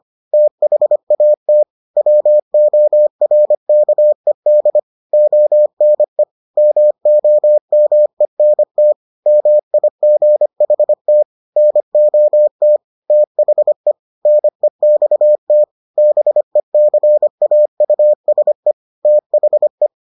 Morse code, heard as a keyed tone, about 25 wpm, 600 Hz. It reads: THAT WORKED ONE MOMENT MIGHT NOT THE NEXT BECAUSE THE